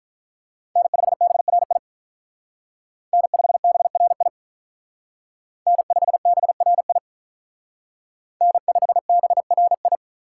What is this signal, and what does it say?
Morse code, heard as a keyed tone, about 41 wpm, 700 Hz. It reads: N5BRI N5BRI N5BRI N5BRI
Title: Speed Racing - USA 1x3 Call Signs 35wpm